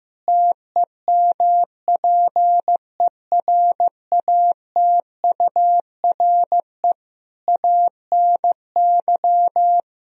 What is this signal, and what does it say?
Morse code, heard as a keyed tone, about 15 wpm, 700 Hz. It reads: TEMPERATURE ANY
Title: Sentences from Top 1000 Words - Review Part 4 - 15wpm